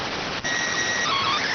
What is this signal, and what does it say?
SSTV (slow-scan television), PD 290.
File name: findings / MR AXIAL 33,99mm 1.gen-29 rx